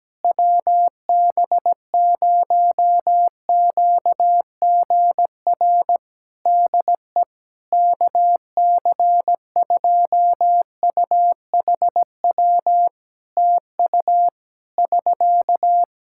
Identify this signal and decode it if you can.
Morse code, heard as a keyed tone, about 17 wpm, 700 Hz. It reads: WB0QGR DE KC2UHW TU <SK>
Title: Practice QSO - Exchange 4 - Ending - 17wpm